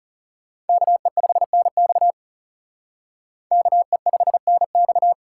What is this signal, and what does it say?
Morse code, heard as a keyed tone, about 36 wpm, 700 Hz. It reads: KE5NX KE5NX
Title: Speed Racing - USA 2x2 Call Signs 30wpm